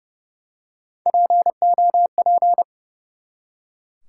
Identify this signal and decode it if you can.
Morse code, heard as a keyed tone, about 30 wpm, 700 Hz. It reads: POP
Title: Three Letter Words - 30wpm